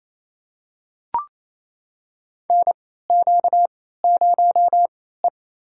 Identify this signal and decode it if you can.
Morse code, heard as a keyed tone, about 27 wpm, spaced as for 16 wpm, 700 Hz. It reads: NQ0E
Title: USA Call Signs - 4 Characters - 3x Character Spacing - 28wpm